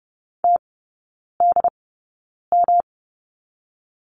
Morse code, heard as a keyed tone, about 30 wpm, spaced as for 9 wpm, 700 Hz. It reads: TDM
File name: Lesson 18e - Letter m - Three Characters 30wpm